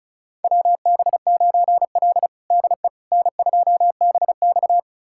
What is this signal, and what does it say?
Morse code, heard as a keyed tone, about 34 wpm, 700 Hz. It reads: WB9L DE N2BX
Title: USA Callsign DE Callsign - 4 Characters - 35wpm